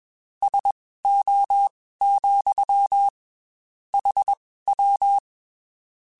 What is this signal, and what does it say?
Morse code, heard as a keyed tone, about 21 wpm, spaced as for 15 wpm, 800 Hz. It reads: SO, HW
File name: QSOs / QSOs-mp3s / WD0FBT 1@20